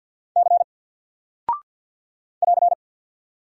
Morse code, heard as a keyed tone, about 49 wpm, 700 Hz. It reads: C <AR>
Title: Single Character (Letter:Number:Punctuation:Prosign) Alt 50wpm